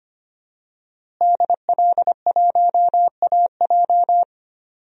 Morse code, heard as a keyed tone, about 25 wpm, 700 Hz. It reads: DL1AJ
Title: DXCC-100 Call Signs - 5 Characters - Repeated 3x - 25wpm